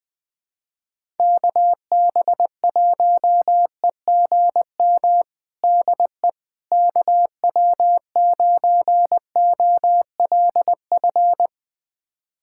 Morse code, heard as a keyed tone, about 20 wpm, 700 Hz. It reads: KB1EGM DE KW9OLF
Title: USA Callsign DE Callsign 20wpm